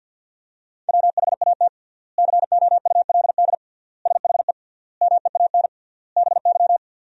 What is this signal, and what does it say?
Morse code, heard as a keyed tone, about 50 wpm, 700 Hz. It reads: WHAT COULD SHE MEAN BY
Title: 3x Word-Spacing - Sentences from Top 200 Words - 50wpm